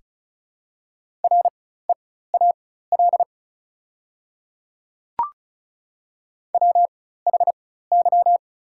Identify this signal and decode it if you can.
Morse code, heard as a keyed tone, about 34 wpm, spaced as for 16 wpm, 700 Hz. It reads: REAL WHY
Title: 4x Character Spacing - Top 300 Words - 35wpm